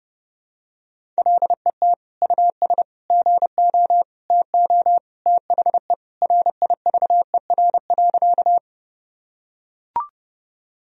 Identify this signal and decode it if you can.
Morse code, heard as a keyed tone, about 30 wpm, 700 Hz. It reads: LET US GO TO THE RIVER.
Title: Sentences from Top 1000 Words - Review Part 1 - 30wpm